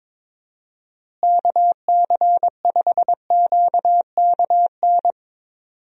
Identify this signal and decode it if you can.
Morse code, heard as a keyed tone, about 22 wpm, 700 Hz. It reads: KC5QKN
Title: USA Call Signs - 6 Characters - Repeated 3x - 22wpm